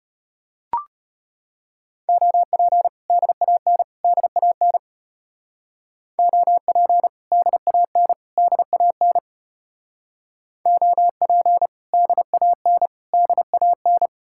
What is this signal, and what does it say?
Morse code, heard as a keyed tone, about 32 wpm, 700 Hz. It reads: OP DAN DAN OP DAN DAN OP DAN DAN